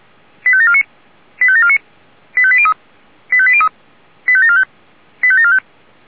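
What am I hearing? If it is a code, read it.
Five-tone selective call (ZVEI), heard as quick runs of five tones, tones of about 70 ms. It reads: 85749 85749 85803 85803 85745 85745
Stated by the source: ZVEI five-tone selective call received via SDR.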